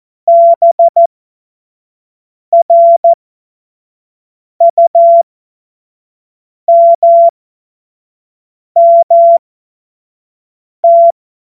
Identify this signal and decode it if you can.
Morse code, heard as a keyed tone, about 14 wpm, spaced as for 5 wpm, 680 Hz. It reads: BRUMMT
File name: Morsekurs / Witze / Witz 28. März 2023 25 F70